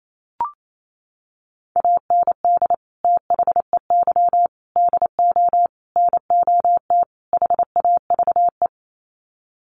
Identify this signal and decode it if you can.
Morse code, heard as a keyed tone, about 28 wpm, 700 Hz. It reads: AND THEY DO NOT HAVE
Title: Sets of 5 Words Encore2 28wpm